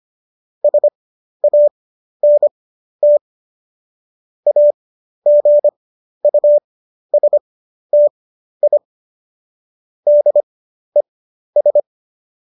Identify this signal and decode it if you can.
Morse code, heard as a keyed tone, about 25 wpm, spaced as for 12 wpm, 590 Hz. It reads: SANT AGUSTI DES
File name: qso009 12